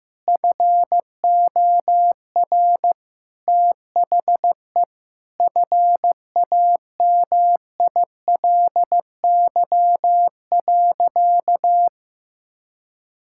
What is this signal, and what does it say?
Morse code, heard as a keyed tone, about 15 wpm, 700 Hz. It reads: FOR THE FAMILY.
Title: Sentences from Top 1000 Words - Review Part 5 - 15wpm